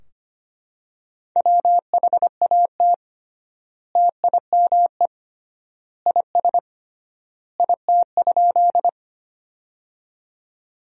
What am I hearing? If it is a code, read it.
Morse code, heard as a keyed tone, about 25 wpm, 700 Hz. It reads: WHAT TIME IS IT?
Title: 3x Word-Spacing - Sentences from Top 100 Words - 25wpm